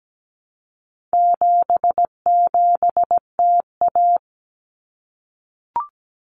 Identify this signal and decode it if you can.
Morse code, heard as a keyed tone, about 17 wpm, 700 Hz. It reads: 77TA
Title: Common Cut Numbers - 4-Digit 17wpm